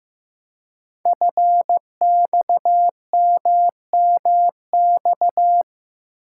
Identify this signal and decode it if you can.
Morse code, heard as a keyed tone, about 15 wpm, 700 Hz. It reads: FXMMX
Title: Code Groups - 5 Letters - 15wpm